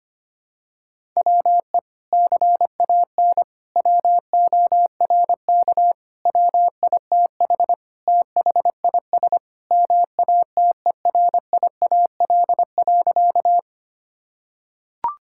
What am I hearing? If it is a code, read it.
Morse code, heard as a keyed tone, about 25 wpm, 700 Hz. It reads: WE CAN WORK WITH THIS MATERIAL.
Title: Sentences from Top 1000 Words - Review Part 5 - 25wpm